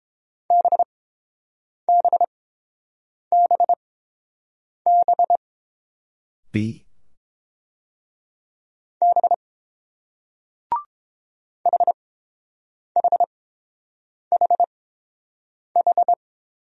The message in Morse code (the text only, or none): BBBB B HHHH